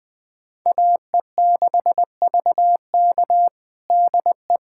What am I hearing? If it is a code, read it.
Morse code, heard as a keyed tone, about 20 wpm, 700 Hz. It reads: AE6VK DE
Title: USA Callsign DE Callsign - 5 Characters - 20wpm